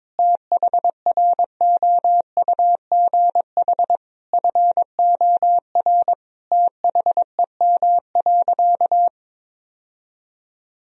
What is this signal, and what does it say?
Morse code, heard as a keyed tone, about 22 wpm, 700 Hz. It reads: THROUGH FOR THEM.
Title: Sentences from Top 1000 Words - Review Part 1 - 22wpm